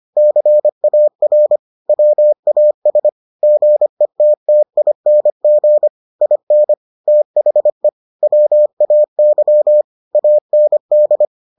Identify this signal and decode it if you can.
Morse code, heard as a keyed tone, about 25 wpm, 600 Hz. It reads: CAR WAS GETTING IN THE WAY AND